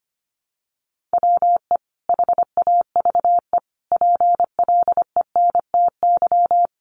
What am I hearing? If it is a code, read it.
Morse code, heard as a keyed tone, about 25 wpm, 700 Hz. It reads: WE HAVE PLENTY
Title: Sentences from Top 1300 Words 25wpm